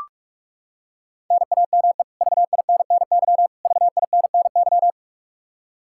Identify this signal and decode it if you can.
Morse code, heard as a keyed tone, about 41 wpm, 700 Hz. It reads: NAME VINNY VINNY
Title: Practice QSO - Exchange 1 - Beginning - 45wpm